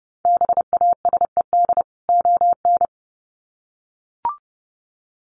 Morse code, heard as a keyed tone, about 30 wpm, 700 Hz. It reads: BASED ON
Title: Sets of 2 Words 30wpm